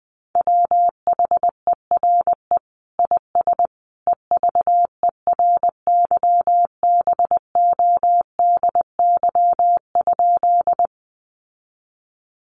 Morse code, heard as a keyed tone, about 20 wpm, 700 Hz. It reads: WHERE IS EVERYBODY?
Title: Sentences from Top 1400 Words 20wpm